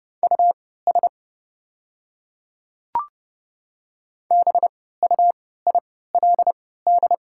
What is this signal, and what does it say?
Morse code, heard as a keyed tone, about 30 wpm, spaced as for 17 wpm, 700 Hz. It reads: US BUILD
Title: 3x Character Spacing - Top 300 Words - 30wpm